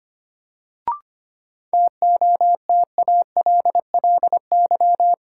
Morse code, heard as a keyed tone, about 25 wpm, 700 Hz. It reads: TOTALLY
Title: Sentences from Top 1600 Words - New Word Drill - 25wpm